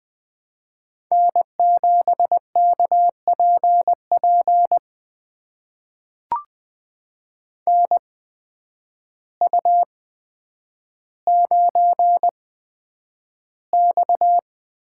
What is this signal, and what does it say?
Morse code, heard as a keyed tone, about 20 wpm, spaced as for 9 wpm, 700 Hz. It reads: N7KPP NU9X
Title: USA Call Signs - 8x Character Spacing - 20wpm